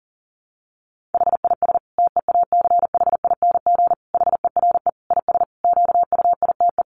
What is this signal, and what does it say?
Morse code, heard as a keyed tone, about 40 wpm, 700 Hz. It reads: HIS TEACHING HERE IS QUITE
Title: Sentences from Top 1100 Words 40wpm